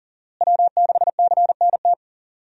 Morse code, heard as a keyed tone, about 40 wpm, 700 Hz. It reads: W6CNT
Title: USA Call Signs - 5 Characters - Repeated 2x - 40wpm